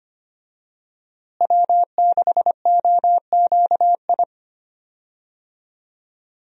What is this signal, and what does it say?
Morse code, heard as a keyed tone, about 25 wpm, 700 Hz. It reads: W6OQI
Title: USA Call Signs - 5 Characters - Repeated 3x - 25wpm